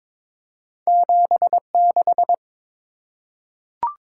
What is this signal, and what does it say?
Morse code, heard as a keyed tone, about 23 wpm, 700 Hz. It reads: 76
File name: practice / 22wpm / Two Letter-Number Combinations - 22wpm